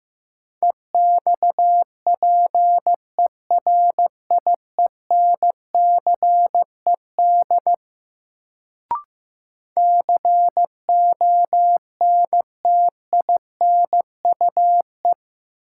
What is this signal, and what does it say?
Morse code, heard as a keyed tone, about 15 wpm, 700 Hz. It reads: EXPERIENCED CONTINUE
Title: Sentences from Top 1100 Words - New Word Drill Alt - 15wpm